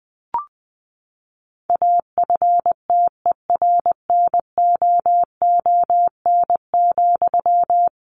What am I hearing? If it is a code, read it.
Morse code, heard as a keyed tone, about 20 wpm, 700 Hz. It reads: AFTERNOON,
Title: Sentences from Top 1500 Words 20wpm